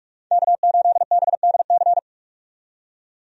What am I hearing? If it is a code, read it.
Morse code, heard as a keyed tone, about 46 wpm, 700 Hz. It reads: K8BDC
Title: USA Call Signs - 5 Characters - 45wpm